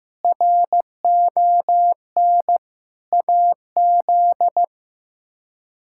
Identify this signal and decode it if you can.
Morse code, heard as a keyed tone, about 15 wpm, 700 Hz. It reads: RON AZ
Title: K1USN SST QSO Practice - 15wpm